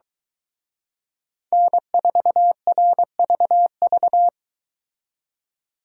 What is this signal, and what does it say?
Morse code, heard as a keyed tone, about 23 wpm, 700 Hz. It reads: N4RVV